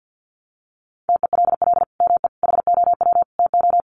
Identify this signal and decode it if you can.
Morse code, heard as a keyed tone, about 53 wpm, 700 Hz. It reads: TELL ME HOW TO
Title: Sentences from Top 1500 Words 50wpm